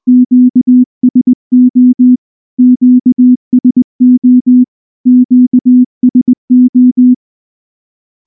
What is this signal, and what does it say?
Morse code, heard as a keyed tone, about 19 wpm, 250 Hz. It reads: QSO QSO QSO
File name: morse / morse-training-20WPM-6